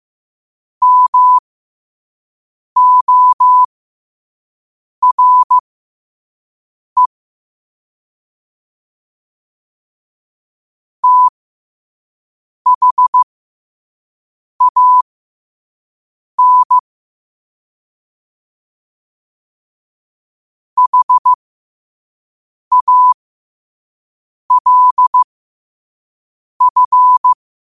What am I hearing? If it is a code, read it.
Morse code, heard as a keyed tone, about 15 wpm, spaced as for 5 wpm, 1000 Hz. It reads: MORE THAN HALF
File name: qotdfiles / 5wpm